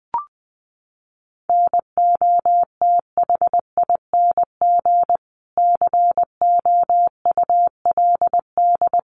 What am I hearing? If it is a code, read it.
Morse code, heard as a keyed tone, about 20 wpm, 700 Hz. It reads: NOTHING COULD
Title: Sentences from Top 1300 Words 20wpm